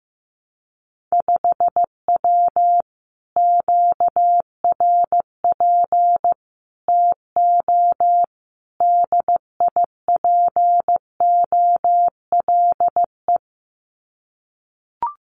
Morse code, heard as a keyed tone, about 15 wpm, 700 Hz. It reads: 5W QRP TO DIPOLE
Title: Common QSO Phrases Encore 15wpm